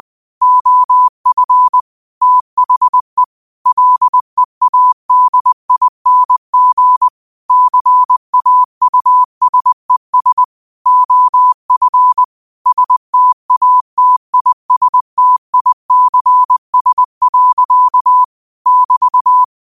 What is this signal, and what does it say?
Morse code, heard as a keyed tone, about 20 wpm, 1000 Hz. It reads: OF THE LEADING CAUSES OF STATISTICS. <BT>
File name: qotdfiles / 20wpm